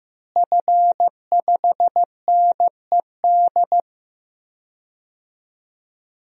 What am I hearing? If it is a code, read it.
Morse code, heard as a keyed tone, about 15 wpm, 700 Hz. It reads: F5NED
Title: DXCC-100 Call Signs - 5 Characters - Repeated 3x - 15wpm